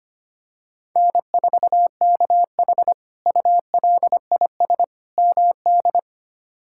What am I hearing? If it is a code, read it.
Morse code, heard as a keyed tone, about 25 wpm, 700 Hz. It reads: N4KH ULIS MD
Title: K1USN SST QSO Practice - 25wpm